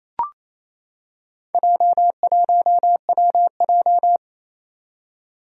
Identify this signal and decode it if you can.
Morse code, heard as a keyed tone, about 27 wpm, 700 Hz. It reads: J1WJ
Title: Four Challenging Letter-Number Combinations 28wpm